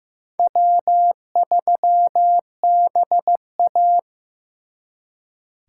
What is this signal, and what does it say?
Morse code, heard as a keyed tone, about 18 wpm, 700 Hz. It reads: W3BA
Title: Speed Racing - USA 1x2 Call Signs 15wpm